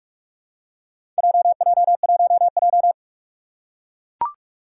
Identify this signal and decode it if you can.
Morse code, heard as a keyed tone, about 44 wpm, 700 Hz. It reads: JJ1J
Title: Four Challenging Letter-Number Combinations 45wpm